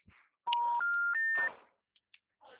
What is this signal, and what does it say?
Special information tone.